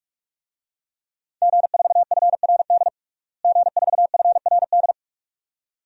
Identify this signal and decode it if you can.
Morse code, heard as a keyed tone, about 47 wpm, 700 Hz. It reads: G4FRD G4FRD
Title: DXCC-100 Call Signs - 5 Characters - Repeated 2x - 45wpm